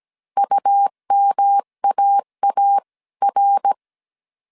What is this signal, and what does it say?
Morse code, heard as a keyed tone, about 17 wpm, 800 Hz. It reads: UMAAR